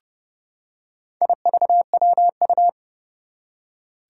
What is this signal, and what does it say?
Morse code, heard as a keyed tone, about 30 wpm, 700 Hz. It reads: IVWU
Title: Four Challenging Letter-Number Combinations 30wpm